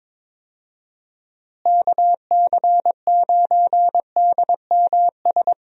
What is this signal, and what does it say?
Morse code, heard as a keyed tone, about 22 wpm, 700 Hz. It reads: KC9DMS
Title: USA Call Signs - 6 Characters - Repeated 3x - 22wpm